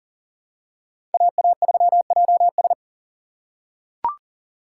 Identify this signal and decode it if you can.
Morse code, heard as a keyed tone, about 40 wpm, 700 Hz. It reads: AA3JS
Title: USA Call Signs - 5 Characters - 4x Character Spacing - 40wpm